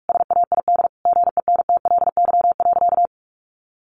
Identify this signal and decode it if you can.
Morse code, heard as a keyed tone, about 44 wpm, 700 Hz. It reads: SAID GENTLY.
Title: Sentences from Top 1500 Words 45wpm